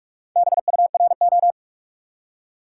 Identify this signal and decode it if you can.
Morse code, heard as a keyed tone, about 47 wpm, 700 Hz. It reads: BURO
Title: Top 90 QSO Elements 45wpm